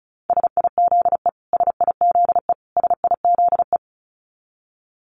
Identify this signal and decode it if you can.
Morse code, heard as a keyed tone, about 34 wpm, 700 Hz. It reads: SIZE SIZE SIZE
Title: Top 500 Words Repeated 35wpm